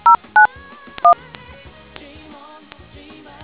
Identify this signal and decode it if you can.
DTMF, heard as telephone touch tones, tones of about 120 ms. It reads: 091